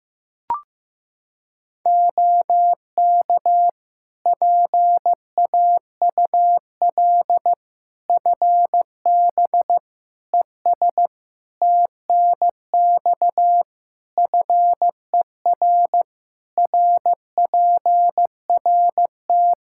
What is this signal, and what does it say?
Morse code, heard as a keyed tone, about 15 wpm, 700 Hz. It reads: OK PAUL FB ES TNX FER RPRT
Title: Practice QSO - Exchange 2 - Info - 15wpm